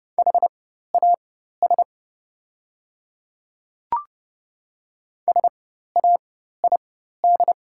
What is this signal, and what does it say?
Morse code, heard as a keyed tone, about 30 wpm, spaced as for 14 wpm, 700 Hz. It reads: HAS SAID
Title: 4x Character Spacing - Top 100 Words - 30wpm